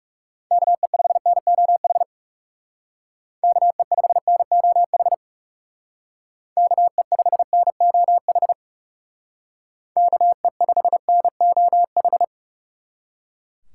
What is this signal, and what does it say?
Morse code, heard as a keyed tone, about 37 wpm, 700 Hz. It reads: KE5NOH KE5NOH KE5NOH KE5NOH
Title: Speed Racing - USA 2x3 Call Signs 30wpm